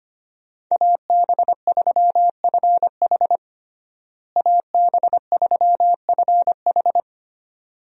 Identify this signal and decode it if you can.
Morse code, heard as a keyed tone, about 25 wpm, 700 Hz. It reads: AB3FH AB3FH
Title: USA Call Signs - 5 Characters - Repeated 2x - 25wpm